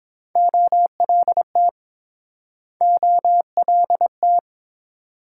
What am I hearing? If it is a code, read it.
Morse code, heard as a keyed tone, about 30 wpm, 700 Hz. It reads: OLT OLT